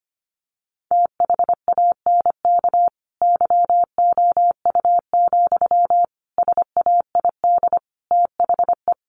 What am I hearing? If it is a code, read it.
Morse code, heard as a keyed tone, about 25 wpm, 700 Hz. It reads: THANK YOU, SAID THE
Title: Sentences from Top 1100 Words 25wpm